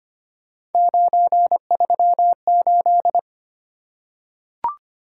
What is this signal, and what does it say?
Morse code, heard as a keyed tone, about 25 wpm, 700 Hz. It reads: 938
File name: Three Letter-Number Combinations - 25wpm